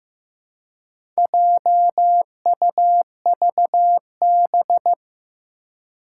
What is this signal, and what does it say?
Morse code, heard as a keyed tone, about 15 wpm, 700 Hz. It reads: JUVB
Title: Four Challenging Letter-Number Combinations 15wpm